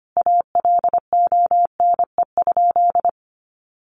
Morse code, heard as a keyed tone, about 25 wpm, 700 Hz. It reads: ALONE?
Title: Sentences from Top 1100 Words 25wpm